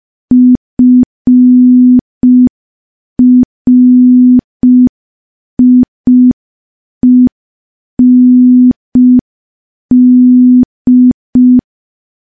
Morse code, heard as a keyed tone, about 5 wpm, 250 Hz. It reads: FRIEND
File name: morse / morse-training-05WPM